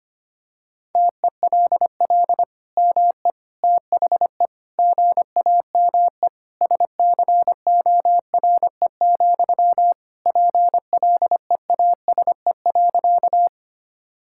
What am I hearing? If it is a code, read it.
Morse code, heard as a keyed tone, about 25 wpm, 700 Hz. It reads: TELL ME THE GAME SCORE, PLEASE.
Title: Sentences from Top 1600 Words 25wpm